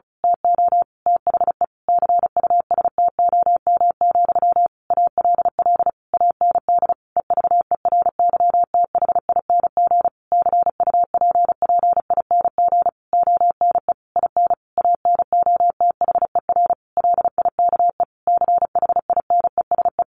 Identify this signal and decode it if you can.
Morse code, heard as a keyed tone, about 34 wpm, 700 Hz. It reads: TO THE CUSTOM, ALL AND EVERYTHING CUPPING ONE IN ANOTHER LIKE CHINESE
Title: A Sound Of Thunder 35wpm